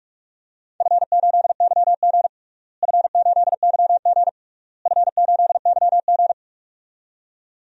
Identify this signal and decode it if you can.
Morse code, heard as a keyed tone, about 42 wpm, 700 Hz. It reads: F8YG F8YG F8YG
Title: DXCC-100 Call Signs - 4 Characters - Repeated 3x - 45wpm